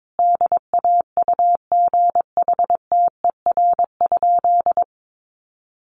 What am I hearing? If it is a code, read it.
Morse code, heard as a keyed tone, about 22 wpm, 700 Hz. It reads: DAUGHTER?
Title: Sentences from Top 1500 Words 22wpm